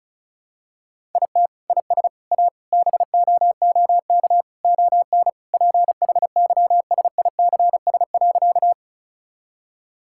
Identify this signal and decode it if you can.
Morse code, heard as a keyed tone, about 36 wpm, 700 Hz. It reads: IT IS A BOOK ON PHYSICS.
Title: Sentences from Top 1000 Words - Review Part 5 - 35wpm